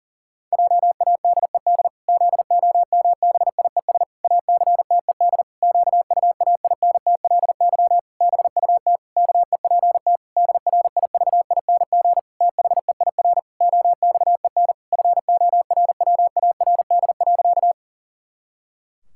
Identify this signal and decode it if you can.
Morse code, heard as a keyed tone, about 37 wpm, 700 Hz. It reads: JADED ZOMBIES ACTED QUAINTLY BUT KEPT DRIVING THEIR OXEN FORWARD.
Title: Pangram Challenge v2 - 40wpm